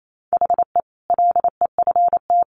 Morse code, heard as a keyed tone, about 28 wpm, 700 Hz. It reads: HE LEFT
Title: Sentences from Top 1100 Words 28wpm